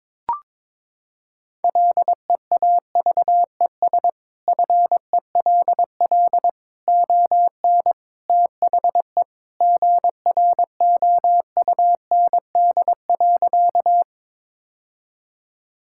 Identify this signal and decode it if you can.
Morse code, heard as a keyed tone, about 22 wpm, 700 Hz. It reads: LEAVES FELL ON THE GROUND.
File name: practice / 22wpm / Sentences with 3-Word Phrases - 22wpm